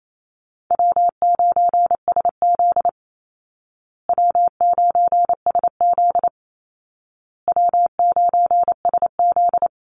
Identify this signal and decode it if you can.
Morse code, heard as a keyed tone, about 28 wpm, 700 Hz. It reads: W9SZ W9SZ W9SZ
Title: Easy Call Signs Repeated 3x - 28wpm